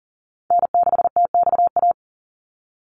Morse code, heard as a keyed tone, about 40 wpm, 700 Hz. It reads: N6TXA
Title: USA Call Signs 40wpm